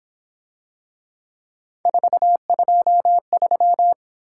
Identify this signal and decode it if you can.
Morse code, heard as a keyed tone, about 26 wpm, 700 Hz. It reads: 423
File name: Speed-Racing - Three Letter-Number Combinations - 22wpm